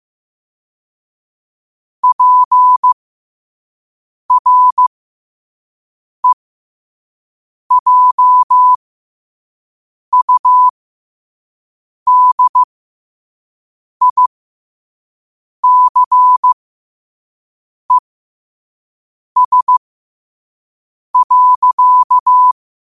Morse code, heard as a keyed tone, about 15 wpm, spaced as for 5 wpm, 1000 Hz. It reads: PREJUDICES.